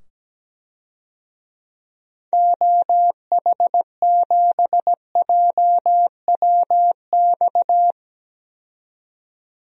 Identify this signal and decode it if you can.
Morse code, heard as a keyed tone, about 17 wpm, 700 Hz. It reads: OH7JWX
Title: DXCC-100 Call Signs - 6 Characters - 17wpm